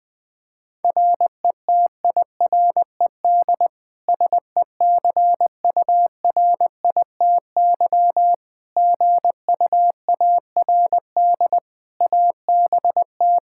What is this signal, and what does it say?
Morse code, heard as a keyed tone, about 20 wpm, 700 Hz. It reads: RETIRED SECURITY GUARD ABT
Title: K1USN SST QSO Practice 20wpm